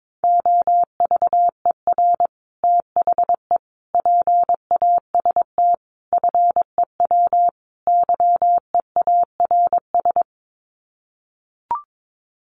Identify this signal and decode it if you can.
Morse code, heard as a keyed tone, about 22 wpm, 700 Hz. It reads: OVER THE PAST FEW YEARS
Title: Sets of 5 Words Encore 22wpm